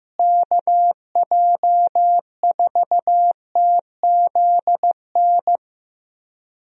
Morse code, heard as a keyed tone, about 15 wpm, 700 Hz. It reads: KJ4TZN
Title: USA Call Signs - 6 Characters - Repeated 2x - 15wpm